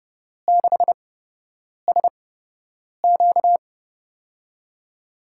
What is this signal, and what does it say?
Morse code, heard as a keyed tone, about 29 wpm, spaced as for 8 wpm, 700 Hz. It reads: 6SQ
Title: USA Call Signs - 4 Characters - 8x Character Spacing - 30wpm